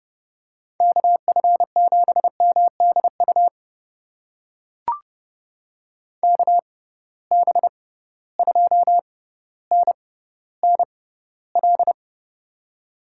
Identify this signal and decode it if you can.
Morse code, heard as a keyed tone, about 30 wpm, spaced as for 15 wpm, 700 Hz. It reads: KF7MDU KB2NNL
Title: USA Call Signs - 6 Characters - 6x Character Spacing - 30wpm